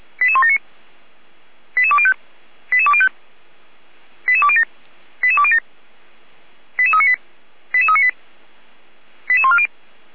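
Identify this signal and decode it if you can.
Five-tone selective call (ZVEI), heard as quick runs of five tones, tones of about 80 ms. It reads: 88179 88205 88205 88207 88207 88398 88398 88140
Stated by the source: ZVEI five-tone call received by SDR, logged by Feuerwehr Orlamünde (fire brigade).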